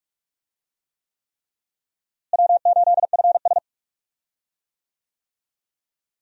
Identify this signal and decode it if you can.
Morse code, heard as a keyed tone, about 43 wpm, 700 Hz. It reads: W8FS
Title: USA Call Signs - 4 Characters - Repeated 3x - 45wpm